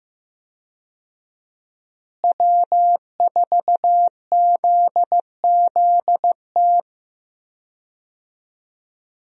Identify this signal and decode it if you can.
Morse code, heard as a keyed tone, about 16 wpm, 700 Hz. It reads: W4ZZT
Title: USA Call Signs - 5 Characters - Repeated 3x - 15wpm